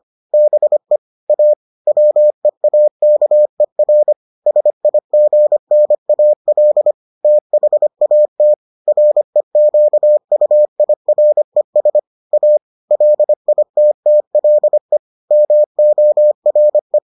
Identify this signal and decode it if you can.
Morse code, heard as a keyed tone, about 25 wpm, 600 Hz. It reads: BE A WEAKER SIGNAL THAT REQUIRES A LITTLE MORE